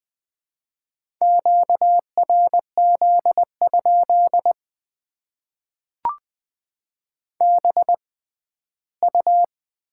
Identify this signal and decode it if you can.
Morse code, heard as a keyed tone, about 21 wpm, spaced as for 14 wpm, 700 Hz. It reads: QRZ? BU
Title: Top 90 QSO Elements - 6x Character Spacing - 20wpm